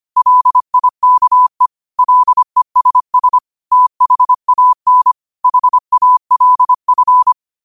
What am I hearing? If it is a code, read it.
Morse code, heard as a keyed tone, about 25 wpm, 1000 Hz. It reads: LIKE LESS THAN HALF
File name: qotdfiles / 25wpm